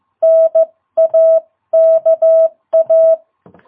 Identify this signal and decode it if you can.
Morse code, heard as a keyed tone, about 15 wpm, 660 Hz. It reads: NAKA
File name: jh8cbh / 2022cw